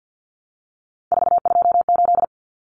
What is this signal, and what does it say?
Morse code, heard as a keyed tone, about 50 wpm, 700 Hz. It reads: 428
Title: Common Cut Numbers - 3-Digit 50wpm